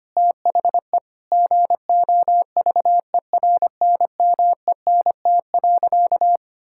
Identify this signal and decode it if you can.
Morse code, heard as a keyed tone, about 25 wpm, 700 Hz. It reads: THE GOVERNMENT.
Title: Sentences from Top 1000 Words - Review Part 5 - 25wpm